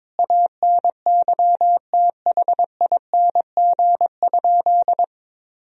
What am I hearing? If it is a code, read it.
Morse code, heard as a keyed tone, about 22 wpm, 700 Hz. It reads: ANYTHING?
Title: Sentences from Top 1000 Words - Review Part 2 - 22wpm